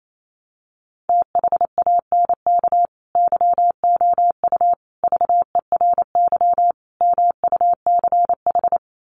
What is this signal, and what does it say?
Morse code, heard as a keyed tone, about 28 wpm, 700 Hz. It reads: THANK YOU VERY MUCH
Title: Sets of 4 Words 28wpm